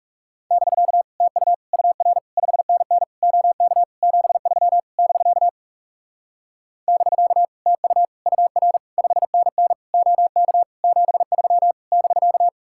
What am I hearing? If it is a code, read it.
Morse code, heard as a keyed tone, about 37 wpm, 700 Hz. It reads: <BK> TU UR 5NN OK 73 <BK> <BK> TU UR 5NN OK 73 <BK>
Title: Speed-Racing Common QSO Phrases Encore - 30wpm